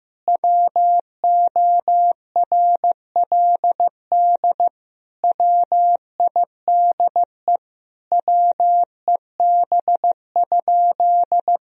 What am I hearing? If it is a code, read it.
Morse code, heard as a keyed tone, about 16 wpm, 700 Hz. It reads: WORLD WIDE WEB?
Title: Sentences from Top 1000 Words - Review Part 4 - 15wpm